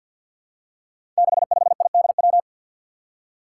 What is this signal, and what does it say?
Morse code, heard as a keyed tone, about 47 wpm, 700 Hz. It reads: 65IDW
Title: Five Challenging Letter-Number Combinations 50wpm